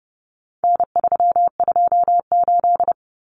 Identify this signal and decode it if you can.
Morse code, heard as a keyed tone, about 30 wpm, 700 Hz. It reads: N328
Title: Common Cut Numbers - 4-Digit 30wpm